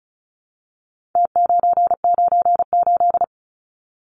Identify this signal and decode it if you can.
Morse code, heard as a keyed tone, about 34 wpm, 700 Hz. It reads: T998
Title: Numbers - Contest Style - 4-Digit 35wpm